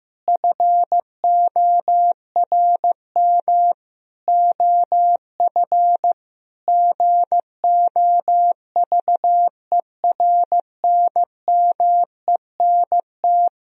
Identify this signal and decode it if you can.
Morse code, heard as a keyed tone, about 15 wpm, 700 Hz. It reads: FORM OF GOVERNMENT
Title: Sentences from Top 1000 Words - Review Part 3 - 15wpm